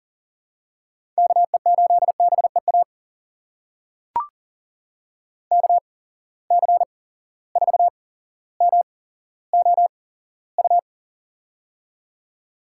Morse code, heard as a keyed tone, about 39 wpm, spaced as for 17 wpm, 700 Hz. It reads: KE8BEA KC4MOU
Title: USA Call Signs - 6 Characters - 8x Character Spacing - 40wpm